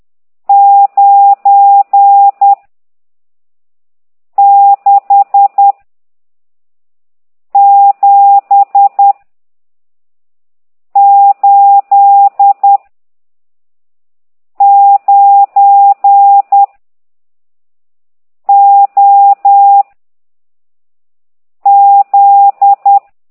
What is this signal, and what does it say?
Morse code, heard as a keyed tone, about 10 wpm, 800 Hz. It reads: 9 6 7 8 9 O Z
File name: ct2idw / exerc15